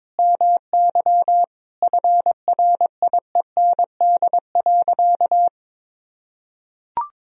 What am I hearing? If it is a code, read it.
Morse code, heard as a keyed tone, about 22 wpm, 700 Hz. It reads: MY FRIEND.
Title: Sentences from Top 1000 Words - Review Part 4 - 22wpm